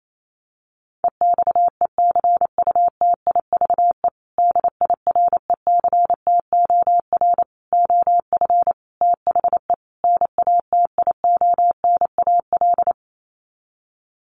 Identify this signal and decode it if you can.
Morse code, heard as a keyed tone, about 28 wpm, 700 Hz. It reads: EXECUTIVE DIRECTOR OF THE NATIONAL
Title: Sets of 5 Words Encore2 28wpm